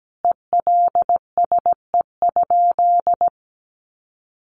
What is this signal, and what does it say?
Morse code, heard as a keyed tone, about 17 wpm, 700 Hz. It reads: ELSE?
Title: Sentences from Top 800 Words 17wpm